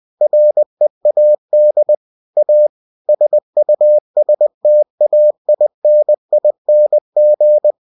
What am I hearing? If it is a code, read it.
Morse code, heard as a keyed tone, about 20 wpm, 600 Hz. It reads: READ A SUSTAINING
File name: cw-books / winnie-the-pooh / winnie-20wpm-page-0019